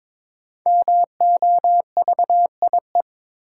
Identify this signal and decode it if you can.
Morse code, heard as a keyed tone, about 22 wpm, 700 Hz. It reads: MOVIE
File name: 5 Letter Words - First 3 Letters Given - 22wpm